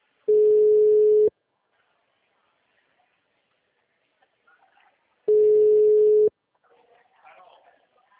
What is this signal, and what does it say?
Ringback tone (Europe, 425 Hz), with speech.
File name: out- -2001- - - .9930